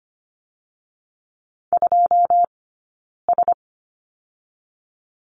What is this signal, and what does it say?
Morse code, heard as a keyed tone, about 25 wpm, spaced as for 9 wpm, 700 Hz. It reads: 2S
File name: course / 25:10wpm / 099 Lesson 17d - Letter u - Two Characters 25wpm